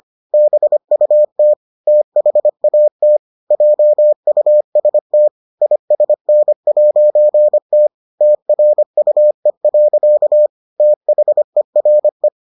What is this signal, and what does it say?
Morse code, heard as a keyed tone, about 25 wpm, 600 Hz. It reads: BUT THAT JUST ISN'T TRUE. THERE